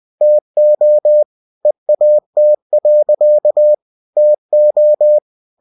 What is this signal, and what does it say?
Morse code, heard as a keyed tone, about 20 wpm, 600 Hz. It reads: TO EAT. TO
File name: cw-books / winnie-the-pooh / winnie-20wpm-page-0079